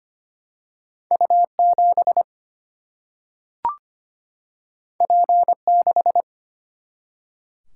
Morse code, heard as a keyed tone, about 25 wpm, 700 Hz. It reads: U7 P6
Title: Single-Letter Single-Number 25wpm